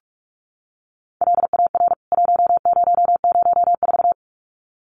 Morse code, heard as a keyed tone, about 44 wpm, 700 Hz. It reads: LAR 1004
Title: CWT Exchange 45wpm